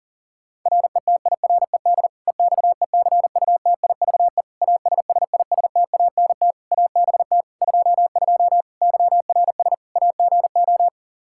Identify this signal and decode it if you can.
Morse code, heard as a keyed tone, about 40 wpm, 700 Hz. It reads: RETIRED EXECUTIVE ASSISTANT ABT 22 YRS AGO
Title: Practice QSO - Exchange 3 - More Info - 40wpm